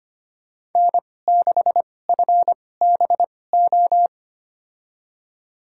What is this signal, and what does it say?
Morse code, heard as a keyed tone, about 25 wpm, spaced as for 18 wpm, 700 Hz. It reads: N6FBO
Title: USA Call Signs - 5 Characters - 2x Character Spacing - 25wpm